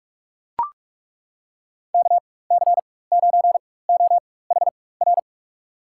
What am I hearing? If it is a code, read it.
Morse code, heard as a keyed tone, about 42 wpm, spaced as for 21 wpm, 700 Hz. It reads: KC9OHR
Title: USA Call Signs - 6 Characters - 4x Character Spacing - 45wpm